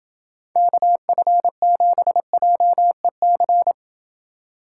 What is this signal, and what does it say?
Morse code, heard as a keyed tone, about 29 wpm, 700 Hz. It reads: KF7JEC